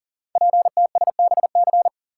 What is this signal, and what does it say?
Morse code, heard as a keyed tone, about 38 wpm, 690 Hz. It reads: PTSBC
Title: Code Groups - 5 Letters - 40wpm